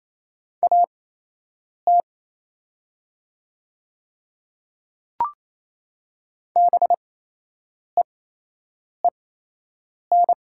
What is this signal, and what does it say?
Morse code, heard as a keyed tone, about 27 wpm, spaced as for 8 wpm, 700 Hz. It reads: AT BEEN
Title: 8x Character Spacing - Top 100 Words - 28wpm